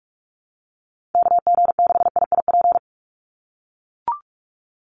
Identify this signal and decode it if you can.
Morse code, heard as a keyed tone, about 44 wpm, 700 Hz. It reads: KG6IIP
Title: Call Signs - 45wpm